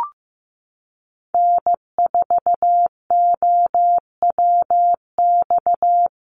Morse code, heard as a keyed tone, about 15 wpm, 700 Hz. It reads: N4OWX
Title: USA Call Signs Repeated 3x 15wpm